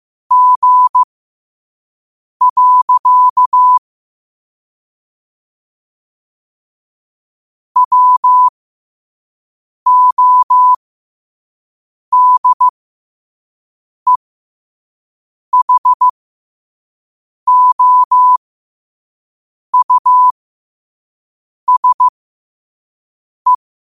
Morse code, heard as a keyed tone, about 15 wpm, spaced as for 5 wpm, 1000 Hz. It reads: G. WODEHOUSE
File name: qotdfiles / 5wpm